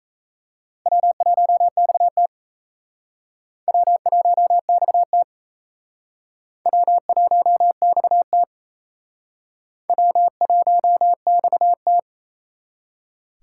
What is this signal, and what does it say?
Morse code, heard as a keyed tone, about 36 wpm, 700 Hz. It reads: W1XT W1XT W1XT W1XT